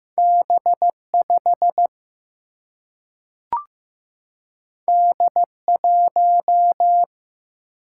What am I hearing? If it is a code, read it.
Morse code, heard as a keyed tone, about 15 wpm, 700 Hz. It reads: B 5 D 1
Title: Single-Letter Single-Number 15wpm